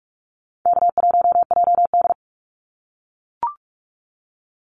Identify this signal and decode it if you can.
Morse code, heard as a keyed tone, about 44 wpm, 700 Hz. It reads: K1JD
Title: NA SOTA Activator Callsigns 45wpm